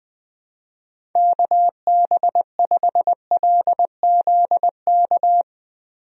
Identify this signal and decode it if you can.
Morse code, heard as a keyed tone, about 20 wpm, 700 Hz. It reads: KB5LZK
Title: USA Call Signs - 6 Characters - Repeated 2x - 20wpm